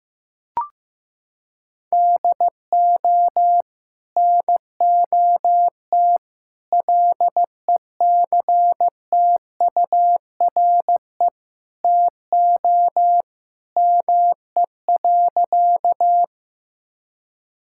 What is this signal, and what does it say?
Morse code, heard as a keyed tone, about 16 wpm, 700 Hz. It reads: DO NOT LECTURE TO ME.
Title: Sentences from Top 1600 Words 15wpm